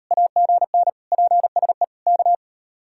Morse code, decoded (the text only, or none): AGN PSE K